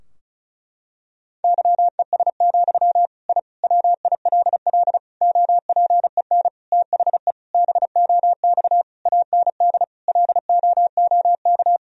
Morse code, heard as a keyed tone, about 36 wpm, 700 Hz. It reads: YES, I WILL OPEN THE BOX AND LOOK
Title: Sentences from Top 1000 Words - Review Part 2 - 35wpm